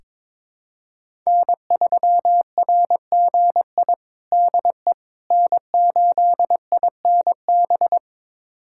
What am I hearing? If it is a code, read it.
Morse code, heard as a keyed tone, about 21 wpm, 700 Hz. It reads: N3RGI DE N8INB
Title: USA Callsign DE Callsign - 5 Characters - 22wpm